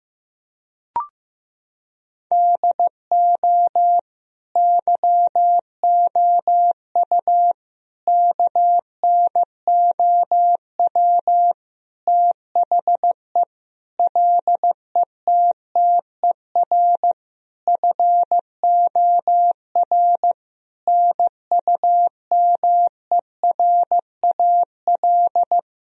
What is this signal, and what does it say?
Morse code, heard as a keyed tone, about 15 wpm, 700 Hz. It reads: DO YOU KNOW THE LETTER FOR NUMERAL
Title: Sentences from Top 1000 Words - Review Part 2 - 15wpm